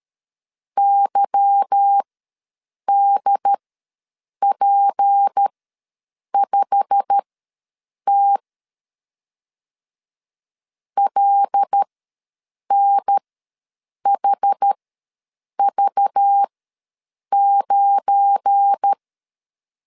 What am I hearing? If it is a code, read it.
Morse code, heard as a keyed tone, about 13 wpm, spaced as for 7 wpm, 800 Hz. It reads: YDP5T LNHV9
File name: LEZIONI / 8 YJ45 / 8 YJ45 ws 12wpm 3146